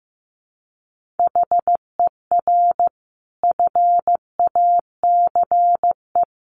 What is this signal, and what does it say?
Morse code, heard as a keyed tone, about 14 wpm, 700 Hz. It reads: HER FACE
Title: Sentences from Top 1200 Words 15wpm